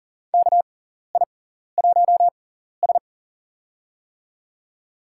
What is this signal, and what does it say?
Morse code, heard as a keyed tone, about 40 wpm, spaced as for 14 wpm, 700 Hz. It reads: KI1S
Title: USA Call Signs - 4 Characters - 6x Character Spacing - 40wpm